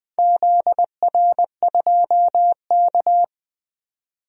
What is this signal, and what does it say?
Morse code, heard as a keyed tone, about 21 wpm, 700 Hz. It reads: ZR2K
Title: DXCC-100 Call Signs - 4 Characters - 20wpm